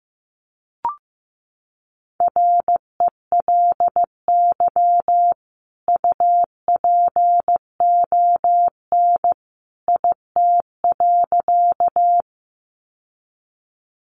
Morse code, heard as a keyed tone, about 15 wpm, 700 Hz. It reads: RELY UPON IT.
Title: Sentences from Top 1200 Words 15wpm